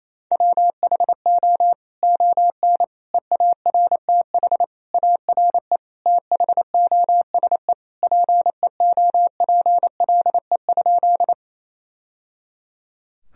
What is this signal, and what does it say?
Morse code, heard as a keyed tone, about 28 wpm, 700 Hz. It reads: WHO ON EARTH ARE THOSE PEOPLE?
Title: Sentences from Top 1000 Words - Review Part 1 - 28wpm